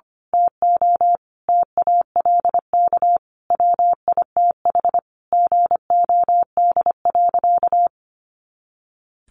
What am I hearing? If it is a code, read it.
Morse code, heard as a keyed tone, about 25 wpm, 700 Hz. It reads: TO TALK WITH GOD.
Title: A Sound Of Thunder 25wpm